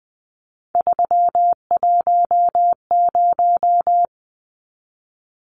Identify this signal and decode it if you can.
Morse code, heard as a keyed tone, about 20 wpm, 700 Hz. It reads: 310
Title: Numbers - 3-Digit 20wpm